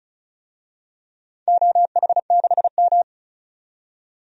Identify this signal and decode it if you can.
Morse code, heard as a keyed tone, about 34 wpm, 700 Hz. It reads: OH6M
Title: DXCC-100 Call Signs - 4 Characters - 35wpm